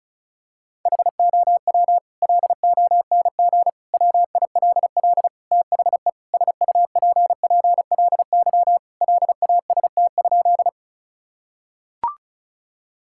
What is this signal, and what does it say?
Morse code, heard as a keyed tone, about 34 wpm, 700 Hz. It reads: HOW LONG WILL THE SUPPLY LAST?
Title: Sentences from Top 1000 Words - Review Part 5 - 35wpm